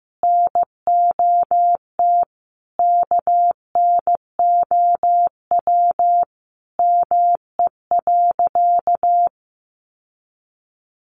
Morse code, heard as a keyed tone, about 14 wpm, 700 Hz. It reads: NOT KNOW ME.
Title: Sentences from Top 600 Words 15wpm